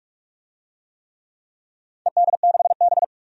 Morse code, heard as a keyed tone, about 47 wpm, 700 Hz. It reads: ED6B
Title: DXCC-100 Call Signs - 4 Characters - 45wpm